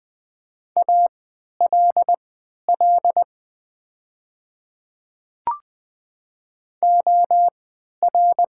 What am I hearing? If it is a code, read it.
Morse code, heard as a keyed tone, about 20 wpm, spaced as for 11 wpm, 700 Hz. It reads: ALL OR
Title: 3x Character Spacing - Top 100 Words - 20wpm